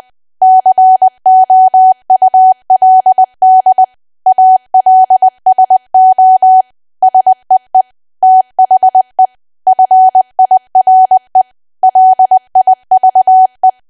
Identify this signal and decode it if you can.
Morse code, heard as a keyed tone, about 20 wpm, 750 Hz. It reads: COULD ALSO SEE THE FIRE LIVE
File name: Archive / 20 WPM / 20WPM